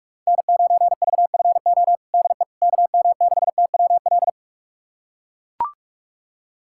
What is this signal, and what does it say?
Morse code, heard as a keyed tone, about 43 wpm, 700 Hz. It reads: N9VFO DE KM6TWL
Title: USA Callsign DE Callsign 45wpm